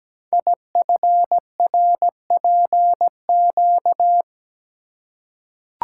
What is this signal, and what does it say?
Morse code, heard as a keyed tone, about 17 wpm, 700 Hz. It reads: IFRPQ
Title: Code Groups - 5 Letter:Numbers - 17wpm